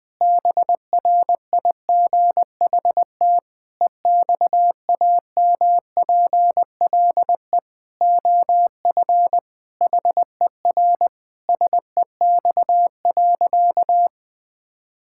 Morse code, heard as a keyed tone, about 20 wpm, 700 Hz. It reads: BRIGHT EXAMPLE OF HER SEX.
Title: Sentences from Top 1600 Words 20wpm